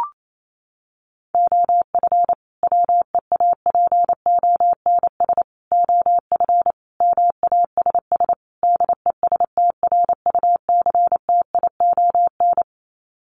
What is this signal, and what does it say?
Morse code, heard as a keyed tone, about 28 wpm, 700 Hz. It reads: OF WEAPONS OF MASS DESTRUCTION
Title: Sets of 5 Words Encore 28wpm